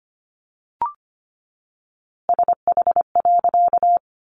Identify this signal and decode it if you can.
Morse code, heard as a keyed tone, about 25 wpm, 700 Hz. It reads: SH.
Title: A Sound Of Thunder 25wpm